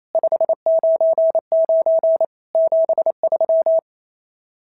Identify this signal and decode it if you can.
Morse code, heard as a keyed tone, about 25 wpm, 650 Hz. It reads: 599 73
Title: SOTA QSO Practice 28wpm